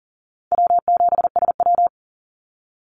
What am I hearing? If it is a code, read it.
Morse code, heard as a keyed tone, about 42 wpm, 700 Hz. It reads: W7SW
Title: Easy Call Signs - 40wpm